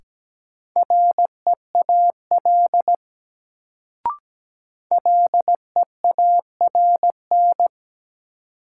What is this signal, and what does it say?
Morse code, heard as a keyed tone, about 17 wpm, 700 Hz. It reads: REAL LEARN
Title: Sentences from Top 300 Words - New Word Drill - 17wpm